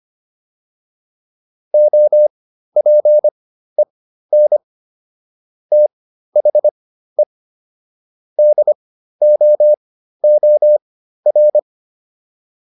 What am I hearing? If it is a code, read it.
Morse code, heard as a keyed tone, about 25 wpm, spaced as for 13 wpm, 600 Hz. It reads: OPEN THE DOOR